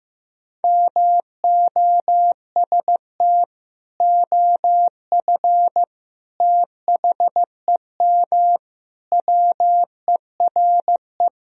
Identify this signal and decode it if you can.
Morse code, heard as a keyed tone, about 16 wpm, 700 Hz. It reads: MOST OF THEM WERE
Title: Sentences from Top 1000 Words - Review Part 1 - 15wpm